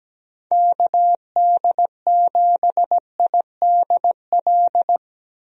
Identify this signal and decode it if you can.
Morse code, heard as a keyed tone, about 17 wpm, 700 Hz. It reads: KD7IDL
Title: USA Call Signs - 6 Characters - Repeated 2x - 17wpm